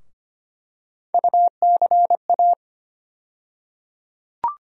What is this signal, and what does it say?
Morse code, heard as a keyed tone, about 25 wpm, 700 Hz. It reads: UCA
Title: USA Call Sign Postfixes - 3 Letters 25wpm